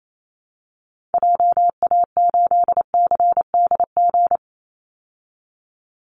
Morse code, heard as a keyed tone, about 28 wpm, 700 Hz. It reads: JA8CDG
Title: Call Signs Repeated 2x - 28wpm